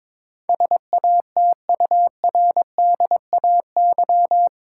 Morse code, heard as a keyed tone, about 22 wpm, 700 Hz. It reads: SATURDAY
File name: practice / 22wpm / Eight Letter Words - 22wpm